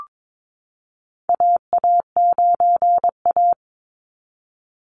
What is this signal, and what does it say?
Morse code, heard as a keyed tone, about 22 wpm, 700 Hz. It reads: AA9A
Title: Call Signs Repeated 2x - 22wpm